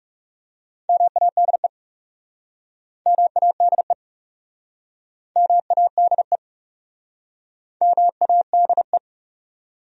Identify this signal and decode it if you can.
Morse code, heard as a keyed tone, about 40 wpm, 700 Hz. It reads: MADE MADE MADE MADE